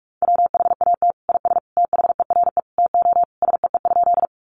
Morse code, heard as a keyed tone, about 44 wpm, 700 Hz. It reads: WHAT IS THERE TO SEE?
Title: Sentences from Top 100 Words 45wpm